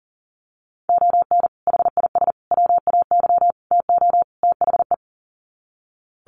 Morse code, heard as a keyed tone, about 37 wpm, 700 Hz. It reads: ON HIS WAY TO THE
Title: Sets of 5 Words Encore2 40wpm